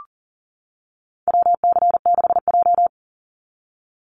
Morse code, heard as a keyed tone, about 40 wpm, 700 Hz. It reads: WC6J
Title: NA SOTA Activator Callsigns 40wpm